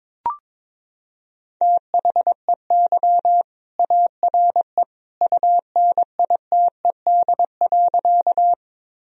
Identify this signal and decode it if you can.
Morse code, heard as a keyed tone, about 22 wpm, 700 Hz. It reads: THEY ARE UNITED.
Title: Sentences from Top 1000 Words - Review Part 4 - 22wpm